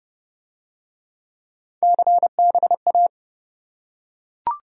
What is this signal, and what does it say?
Morse code, heard as a keyed tone, about 25 wpm, 700 Hz. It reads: CBA